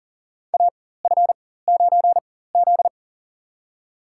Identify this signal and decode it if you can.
Morse code, heard as a keyed tone, about 37 wpm, spaced as for 18 wpm, 700 Hz. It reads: AF9Z
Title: USA Call Signs - 4 Characters - 4x Character Spacing - 40wpm